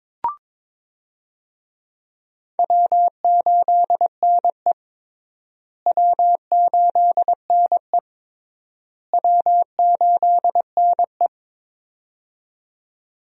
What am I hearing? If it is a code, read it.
Morse code, heard as a keyed tone, about 22 wpm, 700 Hz. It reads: W8NE W8NE W8NE
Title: USA Call Signs - 4 Characters - Repeated 3x - 22wpm